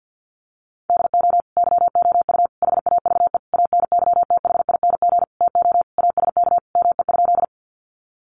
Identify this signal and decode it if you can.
Morse code, heard as a keyed tone, about 50 wpm, 700 Hz. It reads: DO YOU HAVE ANYTHING TO ASK ME?
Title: Sentences from Top 1000 Words 50wpm